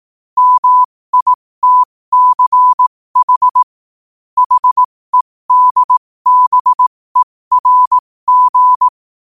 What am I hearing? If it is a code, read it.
Morse code, heard as a keyed tone, about 18 wpm, spaced as for 15 wpm, 1000 Hz. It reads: MITCH HEDBERG
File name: qotdfiles / 15wpm